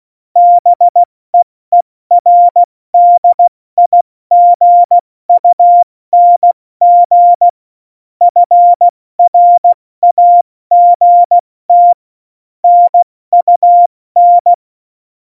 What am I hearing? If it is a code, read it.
Morse code, heard as a keyed tone, about 16 wpm, 700 Hz. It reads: BEERDIGUNG FRAGT NUN
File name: Morsekurs / Witze / Witz 21 Juli 2020, 70 F80